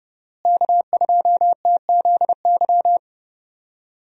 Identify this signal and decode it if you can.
Morse code, heard as a keyed tone, about 30 wpm, 700 Hz. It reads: K2TZY
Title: USA Call Signs - 5 Characters - Repeated 3x - 30wpm